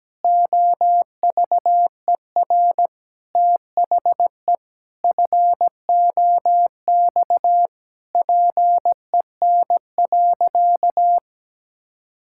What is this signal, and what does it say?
Morse code, heard as a keyed tone, about 17 wpm, 700 Hz. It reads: OVER THE FOX PEN.
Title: Pangram Challenge v2 - 17wpm